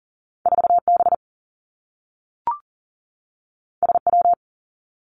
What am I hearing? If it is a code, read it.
Morse code, heard as a keyed tone, about 40 wpm, 700 Hz. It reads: 4B SW
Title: Two Challenging Letter-Number Combinations 40wpm